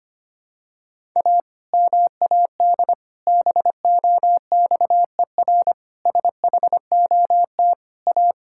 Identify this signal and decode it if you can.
Morse code, heard as a keyed tone, about 25 wpm, 700 Hz. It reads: A MAD BOXER SHOT A
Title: Pangram Challenge v2 - 25wpm